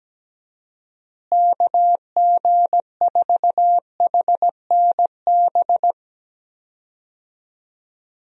Morse code, heard as a keyed tone, about 17 wpm, 700 Hz. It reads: KG4HNB
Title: USA Call Signs - 6 Characters - Repeated 3x - 17wpm